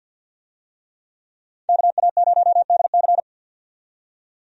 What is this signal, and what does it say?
Morse code, heard as a keyed tone, about 50 wpm, 700 Hz. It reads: KA0DC
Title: USA Call Signs - 5 Characters - 50wpm